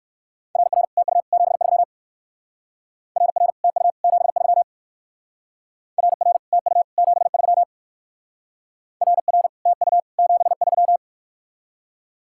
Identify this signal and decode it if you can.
Morse code, heard as a keyed tone, about 59 wpm, 700 Hz. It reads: RR TU 73 RR TU 73 RR TU 73 RR TU 73